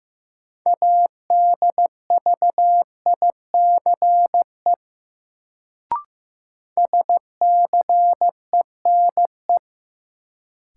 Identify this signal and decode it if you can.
Morse code, heard as a keyed tone, about 15 wpm, 700 Hz. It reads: ADVICE SCENE
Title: Sentences from Top 1100 Words - New Word Drill - 15wpm